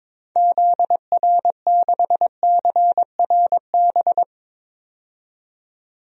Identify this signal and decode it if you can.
Morse code, heard as a keyed tone, about 21 wpm, 700 Hz. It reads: ZR6CRB
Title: DXCC-100 Call Signs - 6 Characters - 22wpm